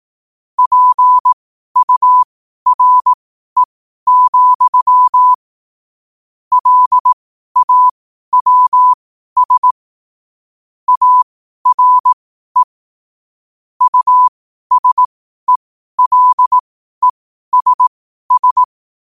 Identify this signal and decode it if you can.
Morse code, heard as a keyed tone, about 18 wpm, spaced as for 12 wpm, 1000 Hz. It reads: PURE, LAWS ARE USELESS
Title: 12 WPM morse code quotes for Thu, 29 Feb 2024